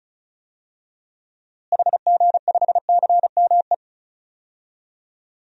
Morse code, heard as a keyed tone, about 36 wpm, 700 Hz. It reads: HG5CME
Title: DXCC-100 Call Signs - 6 Characters - Repeated 3x - 35wpm